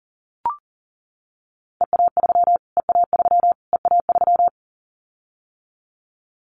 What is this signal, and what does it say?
Morse code, heard as a keyed tone, about 40 wpm, 700 Hz. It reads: EA3 EA3 EA3
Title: Callsign Prefixes for Top 25 DXCC - Repeated 3x - 40wpm